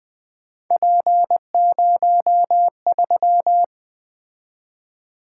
Morse code, heard as a keyed tone, about 20 wpm, 700 Hz. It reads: P03
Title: Three Letter-Number Combinations - 20wpm